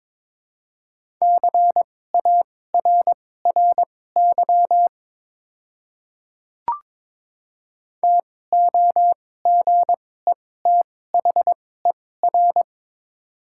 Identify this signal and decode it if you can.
Morse code, heard as a keyed tone, about 22 wpm, spaced as for 16 wpm, 700 Hz. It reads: CARRY TOGETHER
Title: 2x Character Spacing - Top 300 Words - 22wpm